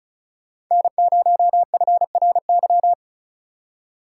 Morse code, heard as a keyed tone, about 43 wpm, 700 Hz. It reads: N0FRY
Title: Speed Racing - USA 1x3 Call Signs 35wpm